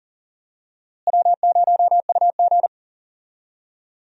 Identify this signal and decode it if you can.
Morse code, heard as a keyed tone, about 40 wpm, 700 Hz. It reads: W0UG
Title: USA Call Signs - 4 Characters - Repeated 3x - 40wpm